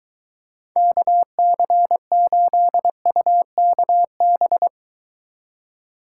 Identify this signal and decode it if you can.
Morse code, heard as a keyed tone, about 23 wpm, 700 Hz. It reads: KC8UKB